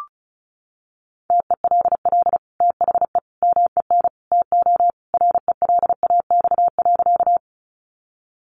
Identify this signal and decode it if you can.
Morse code, heard as a keyed tone, about 34 wpm, 700 Hz. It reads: TELL THE MEN TO RELAX.
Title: Sentences from Top 1400 Words 35wpm